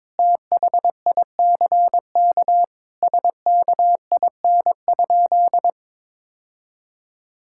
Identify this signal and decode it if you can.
Morse code, heard as a keyed tone, about 21 wpm, 700 Hz. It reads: THICK SKIN?
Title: Sentences from Top 1600 Words 22wpm